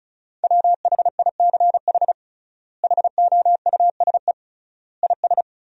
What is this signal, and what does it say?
Morse code, heard as a keyed tone, about 36 wpm, 700 Hz. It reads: WHICH HOUSE IS
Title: 3x Word-Spacing - Sentences from Top 200 Words - 35wpm